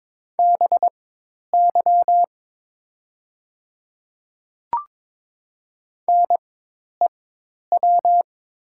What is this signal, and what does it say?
Morse code, heard as a keyed tone, about 22 wpm, spaced as for 10 wpm, 700 Hz. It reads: BY NEW
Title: 4x Character Spacing - Top 300 Words - 22wpm